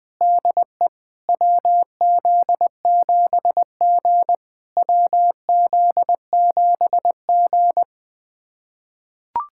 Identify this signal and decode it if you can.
Morse code, heard as a keyed tone, about 21 wpm, 700 Hz. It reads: DE WZ7G WZ7G
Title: SOTA QSO Practice 20wpm